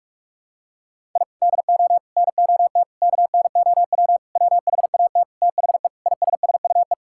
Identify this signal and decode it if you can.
Morse code, heard as a keyed tone, about 42 wpm, 690 Hz. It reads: I DO NOT KNOW WHAT THE ISSUE
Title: Sentences from Top 1000 Words - Review Part 6 - 45wpm